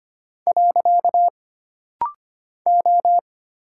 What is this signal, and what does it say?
Morse code, heard as a keyed tone, about 25 wpm, 700 Hz. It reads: . O
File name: Single Character (Letter:Number:Punctuation:Prosign) - Rapid-Fire - 25wpm